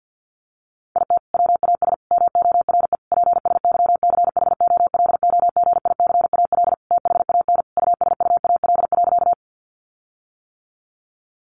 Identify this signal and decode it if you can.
Morse code, heard as a keyed tone, about 50 wpm, 700 Hz. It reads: IT WAS MORE PSYCHOLOGICAL THAN USUAL.
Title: Sentences from Top 1400 Words 50wpm